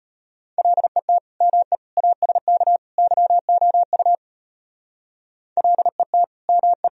Morse code, heard as a keyed tone, about 31 wpm, 700 Hz. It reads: LET ME ASK YOU LET ME
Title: Speed-Racing - Sets of 4 Words 25wpm